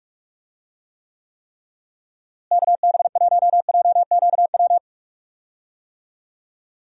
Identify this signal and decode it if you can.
Morse code, heard as a keyed tone, about 41 wpm, 700 Hz. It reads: KB1JQW
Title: USA Call Signs - 6 Characters - Repeated 3x - 45wpm